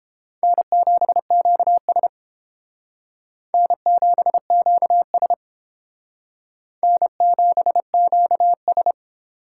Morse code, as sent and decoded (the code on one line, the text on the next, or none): -. --... --.- ... / -. --... --.- ... / -. --... --.- ...
N7QS N7QS N7QS